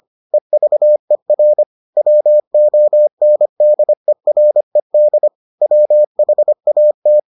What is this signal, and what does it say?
Morse code, heard as a keyed tone, about 25 wpm, 600 Hz. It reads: EVER WONDERED WHAT